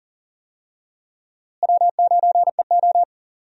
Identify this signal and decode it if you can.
Morse code, heard as a keyed tone, about 40 wpm, 700 Hz. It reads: W9EO
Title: USA Call Signs - 4 Characters - Repeated 3x - 40wpm